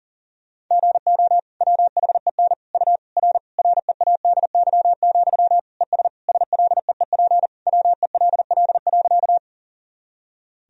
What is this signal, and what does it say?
Morse code, heard as a keyed tone, about 40 wpm, 700 Hz. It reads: GO WHEN U R READY, ES SLEEP WELL.
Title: Sentences from Top 500 Words and Abbreviations 40wpm